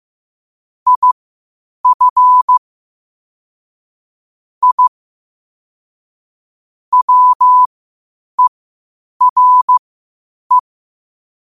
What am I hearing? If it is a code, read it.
Morse code, heard as a keyed tone, about 15 wpm, spaced as for 8 wpm, 1000 Hz. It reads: IF I WERE
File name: qotdfiles / 8wpm